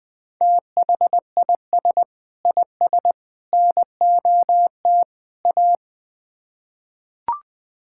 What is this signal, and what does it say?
Morse code, heard as a keyed tone, about 20 wpm, 700 Hz. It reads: THIS IS NOT A
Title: Build A Phrase - 4-Word Phrases - 20wpm